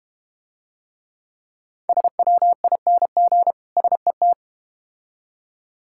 Morse code, heard as a keyed tone, about 32 wpm, 700 Hz. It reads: SWING SET